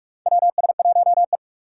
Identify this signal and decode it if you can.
Morse code, heard as a keyed tone, about 41 wpm, 700 Hz. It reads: WS1E
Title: USA Call Signs - 4 Characters - 3x Character Spacing - 45wpm